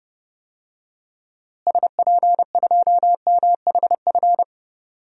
Morse code, heard as a keyed tone, about 32 wpm, 700 Hz. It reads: SP2MHF
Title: DXCC-100 Call Signs - 6 Characters - Repeated 2x - 30wpm